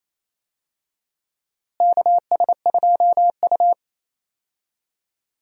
Morse code, heard as a keyed tone, about 28 wpm, 700 Hz. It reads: KS2U
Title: USA Call Signs - 4 Characters - Repeated 2x - 28wpm